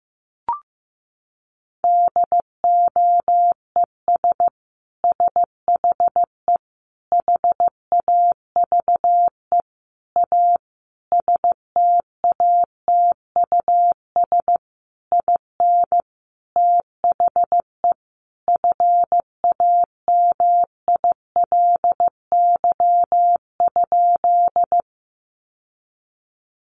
Morse code, heard as a keyed tone, about 15 wpm, 700 Hz. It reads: DOES SHE HAVE A STATUS IN THE FAMILY?
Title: Sentences from Top 1400 Words 15wpm